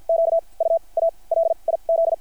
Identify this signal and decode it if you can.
Morse code, heard as a keyed tone, about 50 wpm, spaced as for 31 wpm, 660 Hz. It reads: QUALIZ
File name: archived / CRSnet